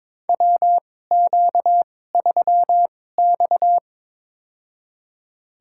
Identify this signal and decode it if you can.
Morse code, heard as a keyed tone, about 22 wpm, spaced as for 16 wpm, 700 Hz. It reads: WQ3X
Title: USA Call Signs - 4 Characters - 2x Character Spacing - 22wpm